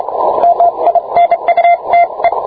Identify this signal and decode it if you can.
Morse code, heard as a keyed tone, about 27 wpm, 690 Hz, with radio noise. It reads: MINUTE